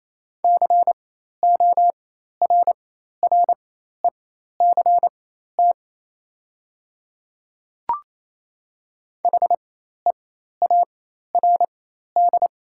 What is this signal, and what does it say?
Morse code, heard as a keyed tone, about 28 wpm, spaced as for 13 wpm, 700 Hz. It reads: CORRECT HEARD
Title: 4x Character Spacing - Top 500 Words - 28wpm